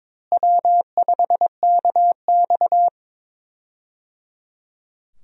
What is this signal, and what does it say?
Morse code, heard as a keyed tone, about 22 wpm, 700 Hz. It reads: W5KX
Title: USA Call Signs - 4 Characters - Repeated 3x - 22wpm